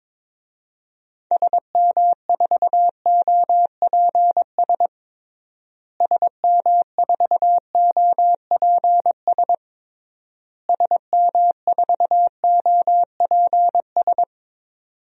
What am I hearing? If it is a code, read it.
Morse code, heard as a keyed tone, about 22 wpm, 700 Hz. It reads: SM4OPS SM4OPS SM4OPS
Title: DXCC-100 Call Signs - 6 Characters - Repeated 3x - 22wpm